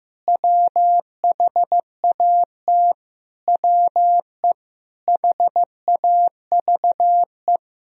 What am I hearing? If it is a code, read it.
Morse code, heard as a keyed tone, about 15 wpm, 700 Hz. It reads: WHAT WE HAVE
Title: Sentences from Top 1000 Words - Review Part 3 - 15wpm